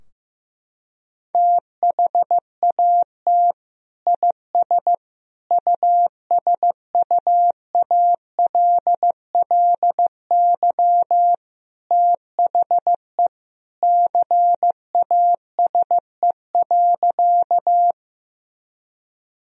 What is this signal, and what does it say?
Morse code, heard as a keyed tone, about 16 wpm, 700 Hz. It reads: THAT IS USUALLY THE CASE.
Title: Sentences from Top 1000 Words - Review Part 3 - 15wpm